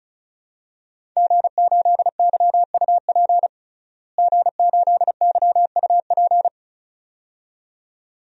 Morse code, heard as a keyed tone, about 34 wpm, 700 Hz. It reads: G8YUP G8YUP
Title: DXCC-100 Call Signs - 5 Characters - Repeated 2x - 35wpm